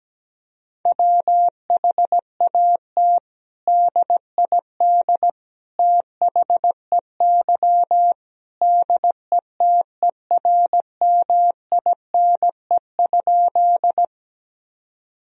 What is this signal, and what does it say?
Morse code, heard as a keyed tone, about 17 wpm, 700 Hz. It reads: WHAT DID THEY DETERMINE?
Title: Sentences from Top 1000 Words - Review Part 4 - 17wpm